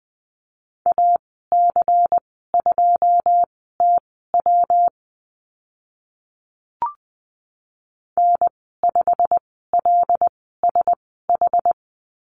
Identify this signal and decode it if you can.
Morse code, heard as a keyed tone, about 20 wpm, spaced as for 14 wpm, 700 Hz. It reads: AC2TW N5LSH
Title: 122 Lesson 20g - Number 3 - Call Signs 20wpm